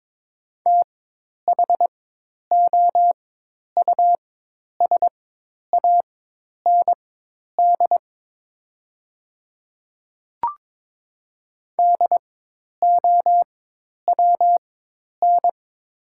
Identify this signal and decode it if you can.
Morse code, heard as a keyed tone, about 22 wpm, spaced as for 10 wpm, 700 Hz. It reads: THOUSAND DOWN
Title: 4x Character Spacing - Top 500 Words - 22wpm